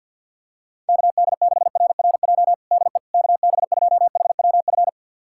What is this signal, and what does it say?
Morse code, heard as a keyed tone, about 52 wpm, 700 Hz. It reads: KD6RRJ DE KB2HWF
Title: USA Callsign DE Callsign - 6 Characters - 50wpm